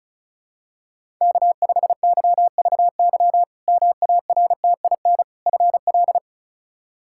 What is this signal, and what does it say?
Morse code, heard as a keyed tone, about 34 wpm, 700 Hz. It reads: K5YVY MARTIN FL
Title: K1USN SST QSO Practice - 35wpm